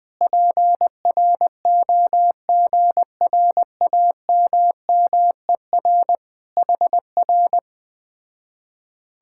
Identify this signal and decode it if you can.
Morse code, heard as a keyed tone, about 20 wpm, 700 Hz. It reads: PROGRAMMER HR